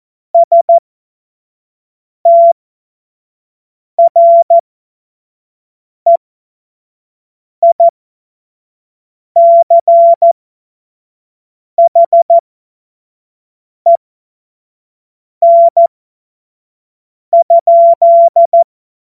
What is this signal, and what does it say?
Morse code, heard as a keyed tone, about 14 wpm, spaced as for 5 wpm, 680 Hz. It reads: STREICHEN?
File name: Morsekurs / Witze / Witz 12 Dezember 2023 25 F70